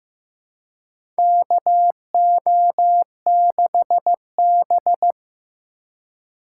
Morse code, heard as a keyed tone, about 15 wpm, 700 Hz. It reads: KO6B
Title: USA Call Signs - 4 Characters - 6x Character Spacing - 15wpm